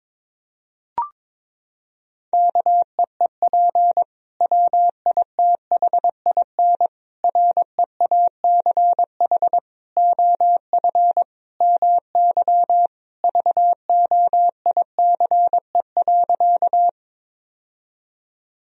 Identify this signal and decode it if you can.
Morse code, heard as a keyed tone, about 22 wpm, 700 Hz. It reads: KEEP WITHIN REACH OF MY VOICE.
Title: Sentences from Top 1000 Words - Review Part 4 - 22wpm